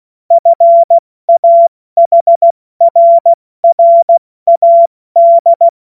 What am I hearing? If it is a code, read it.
Morse code, heard as a keyed tone, about 16 wpm, spaced as for 14 wpm, 680 Hz. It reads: FAHRRAD
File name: Morsekurs / Witze / Witz 13 Mai 2025 70 F80